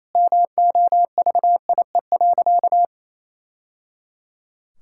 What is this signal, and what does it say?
Morse code, heard as a keyed tone, about 28 wpm, 700 Hz. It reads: MOVIE.
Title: Sentences from Top 1000 Words - Review Part 5 - 28wpm